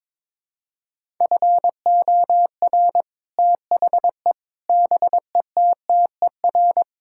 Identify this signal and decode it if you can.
Morse code, heard as a keyed tone, about 22 wpm, 700 Hz. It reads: FOR THE BETTER
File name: practice / 22wpm / Sets of 6 Words - 22wpm